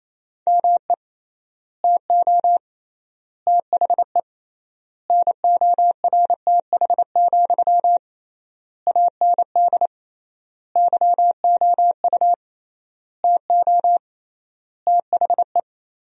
Morse code, heard as a keyed tone, about 28 wpm, 700 Hz. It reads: ME TO THE NORTH, AND YOU TO THE
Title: 3x Word-Spacing - Sentences from Top 400 Words - 28wpm